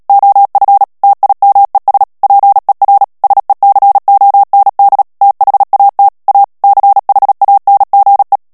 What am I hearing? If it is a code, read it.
Morse code, heard as a keyed tone, about 37 wpm, 800 Hz. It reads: OF TIMES PER SECOND THAT A CHANGE
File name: morse mp3 / ANGLAIS texte clair 3 vitesse 35 mots minute